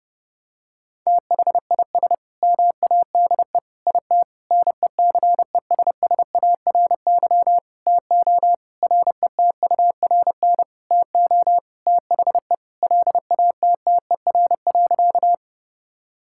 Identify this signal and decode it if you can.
Morse code, heard as a keyed tone, about 30 wpm, 700 Hz. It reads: THIS MADE IT NECESSARY TO RETURN TO THE LATTER.
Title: Sentences from Top 1600 Words 30wpm